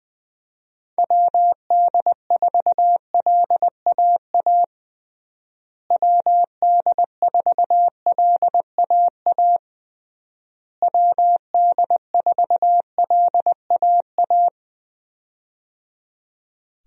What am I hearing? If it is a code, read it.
Morse code, heard as a keyed tone, about 20 wpm, 700 Hz. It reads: WD4LAA WD4LAA WD4LAA
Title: USA Call Signs - 6 Characters - Repeated 3x - 20wpm